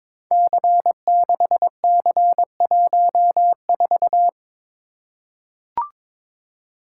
Morse code, heard as a keyed tone, about 22 wpm, 700 Hz. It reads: C6C14
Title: Code Groups - 5 Letter:Numbers - 22wpm